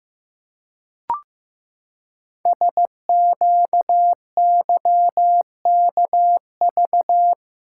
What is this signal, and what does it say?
Morse code, heard as a keyed tone, about 15 wpm, 700 Hz. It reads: SQYKV
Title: Code Groups - 5 Letters - 15wpm